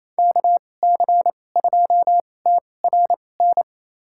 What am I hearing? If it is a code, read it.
Morse code, heard as a keyed tone, about 28 wpm, spaced as for 20 wpm, 700 Hz. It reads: KC2TRN
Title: USA Call Signs - 6 Characters - 2x Character Spacing - 28wpm